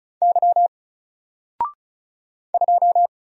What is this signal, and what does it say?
Morse code, heard as a keyed tone, about 35 wpm, 700 Hz. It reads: Y 2
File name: Single-Letter-Number-35wpm